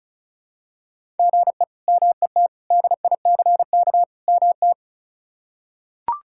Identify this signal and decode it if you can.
Morse code, heard as a keyed tone, about 36 wpm, 700 Hz. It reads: GE MET DICK MT
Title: K1USN SST QSO Practice - 35wpm